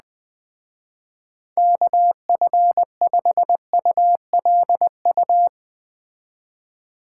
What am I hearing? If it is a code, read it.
Morse code, heard as a keyed tone, about 20 wpm, 700 Hz. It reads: KF5ULU